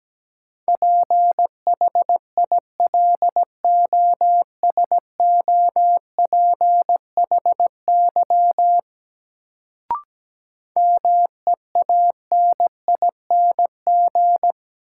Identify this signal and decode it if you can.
Morse code, heard as a keyed tone, about 17 wpm, 700 Hz. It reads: PHILOSOPHY MEANING
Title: Sentences from Top 900 Words - New Word Drill - 17wpm